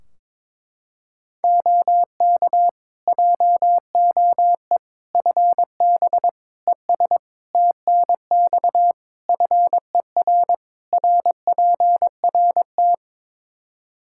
Morse code, heard as a keyed tone, about 22 wpm, 700 Hz. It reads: OK JOE FB ES TNX FER RPRT
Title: K1USN SST QSO Practice 22wpm